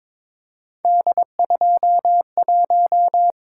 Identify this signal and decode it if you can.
Morse code, heard as a keyed tone, about 22 wpm, 700 Hz. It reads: D21
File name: practice / 22wpm / Three Letter-Number Combinations - 22wpm